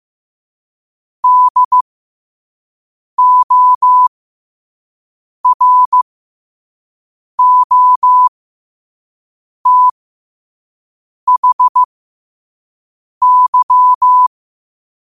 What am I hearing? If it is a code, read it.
Morse code, heard as a keyed tone, about 15 wpm, spaced as for 5 wpm, 1000 Hz. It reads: DOROTHY
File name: qotdfiles / 5wpm